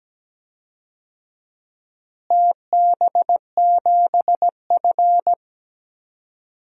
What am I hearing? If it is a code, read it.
Morse code, heard as a keyed tone, about 17 wpm, 700 Hz. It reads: TB7F
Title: DXCC-100 Call Signs - 4 Characters - Repeated 2x - 17wpm